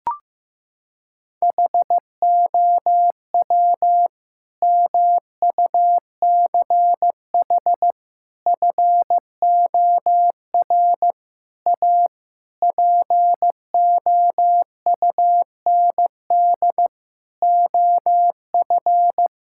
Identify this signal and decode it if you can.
Morse code, heard as a keyed tone, about 15 wpm, 700 Hz. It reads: HOW MUCH FOR A POUND OF
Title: Sentences from Top 1000 Words - Review Part 5 - 15wpm